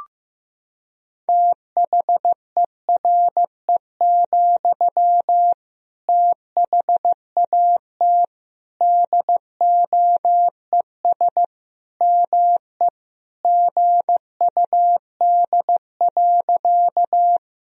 Morse code, heard as a keyed tone, about 15 wpm, 700 Hz. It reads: THERE, THAT DOES ME GUD.
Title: Sentences from Top 200 Words and Abbreviations 15wpm